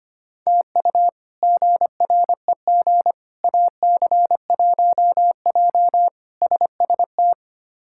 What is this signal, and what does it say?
Morse code, heard as a keyed tone, about 25 wpm, 700 Hz. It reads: TU GREG AC1J SST
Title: K1USN SST QSO Practice - 25wpm